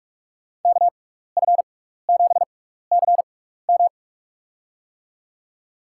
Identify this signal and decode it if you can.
Morse code, heard as a keyed tone, about 47 wpm, spaced as for 16 wpm, 700 Hz. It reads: KF7CM
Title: USA Call Signs - 5 Characters - 6x Character Spacing - 45wpm